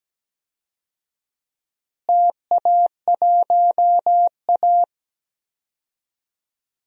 Morse code, heard as a keyed tone, about 17 wpm, 700 Hz. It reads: TA1A
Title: DXCC-100 Call Signs - 4 Characters - Repeated 3x - 17wpm